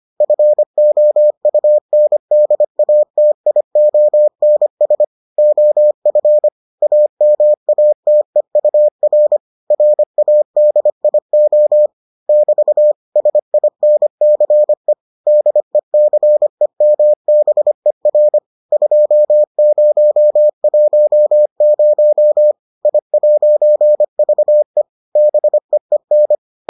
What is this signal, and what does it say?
Morse code, heard as a keyed tone, about 25 wpm, 600 Hz. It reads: FOUNDATIONS OF AMATEUR RADIO <BT> SINCE DECEMBER 2010 I'VE BEEN